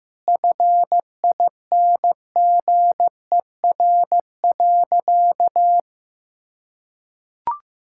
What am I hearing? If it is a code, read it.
Morse code, heard as a keyed tone, about 15 wpm, 700 Hz. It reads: FINGER.
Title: Sentences from Top 1600 Words 15wpm